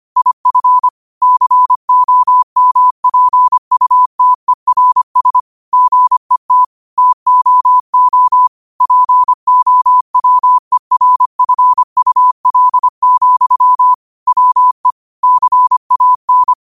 Morse code, heard as a keyed tone, about 25 wpm, 1000 Hz. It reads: IF COMPUTERS GET TOO POWERFUL, WE CAN
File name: qotdfiles / 25wpm